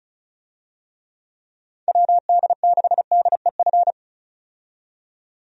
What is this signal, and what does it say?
Morse code, heard as a keyed tone, about 36 wpm, 700 Hz. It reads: WD6DEF
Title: USA Call Signs - 6 Characters - 35wpm